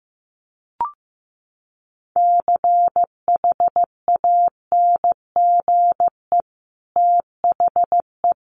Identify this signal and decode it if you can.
Morse code, heard as a keyed tone, about 15 wpm, 700 Hz. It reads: CHANGE THE
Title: Sentences from Top 800 Words 15wpm